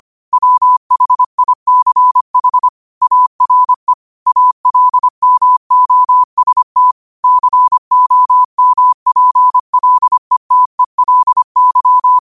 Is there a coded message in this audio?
Yes: Morse code from 0.3 to 12.2 s, heard as a keyed tone, about 25 wpm, 1000 Hz. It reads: WHICH ARE ALMOST COMPLETELY